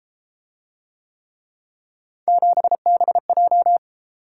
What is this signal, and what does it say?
Morse code, heard as a keyed tone, about 30 wpm, 700 Hz. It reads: 7BJ